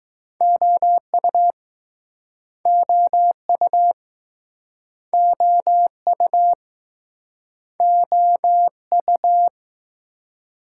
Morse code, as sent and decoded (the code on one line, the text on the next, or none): --- ..- / --- ..- / --- ..- / --- ..-
OU OU OU OU